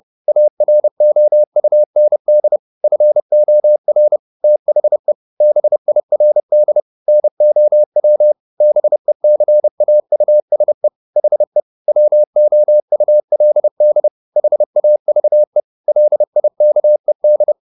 Morse code, heard as a keyed tone, about 30 wpm, 600 Hz. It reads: AROUND FOR THE BIRD NOW BECAUSE HE WOULD HAVE LIKED
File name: the old man and the sea / the old man and the sea 1 30wpm